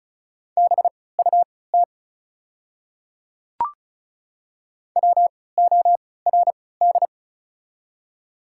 Morse code, heard as a keyed tone, about 36 wpm, spaced as for 20 wpm, 690 Hz. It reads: BUT WORD
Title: 3x Character Spacing - Top 100 Words - 35wpm